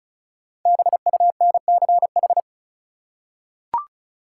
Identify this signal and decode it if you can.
Morse code, heard as a keyed tone, about 35 wpm, 700 Hz. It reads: BUNCH
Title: Five Letter Words - 35wpm